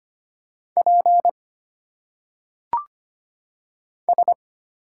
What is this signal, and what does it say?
Morse code, heard as a keyed tone, about 25 wpm, 700 Hz. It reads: P S
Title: Single Letters - Phonetic Pronunciation 25wpm